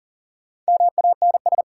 Morse code, heard as a keyed tone, about 38 wpm, 700 Hz. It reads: MANS
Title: Sentences from Top 1000 Words - Review Part 6 - New Word Drill - 40wpm